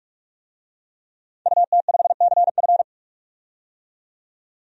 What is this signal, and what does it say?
Morse code, heard as a keyed tone, about 46 wpm, 700 Hz. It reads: UT5CF
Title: DXCC-100 Call Signs - 5 Characters - 45wpm